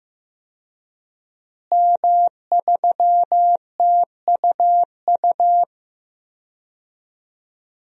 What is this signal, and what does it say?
Morse code, heard as a keyed tone, about 16 wpm, 700 Hz. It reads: M3TUU
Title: DXCC-100 Call Signs - 5 Characters - Repeated 2x - 15wpm